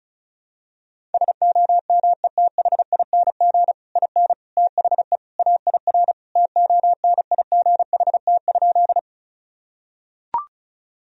Morse code, heard as a keyed tone, about 34 wpm, 700 Hz. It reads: SOMETHING IN THE AIR TONIGHT?
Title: Sentences from Top 1000 Words - Review Part 1 - 35wpm